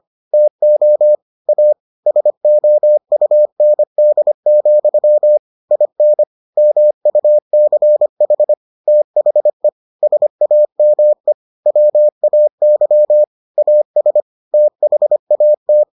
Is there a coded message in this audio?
Yes: Morse code from 0.3 to 15.8 s, heard as a keyed tone, about 25 wpm, 600 Hz. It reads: TO A SOUND, IN MUCH THE SAME WAY AS THAT